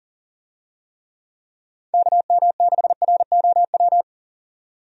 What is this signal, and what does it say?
Morse code, heard as a keyed tone, about 40 wpm, 700 Hz. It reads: KM6ROW
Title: USA Call Signs - 6 Characters - 40wpm